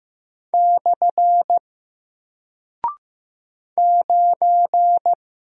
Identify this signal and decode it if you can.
Morse code, heard as a keyed tone, about 15 wpm, 700 Hz. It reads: / 9
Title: Single Character (Letter:Number:Punctuation:Prosign) Alt 15wpm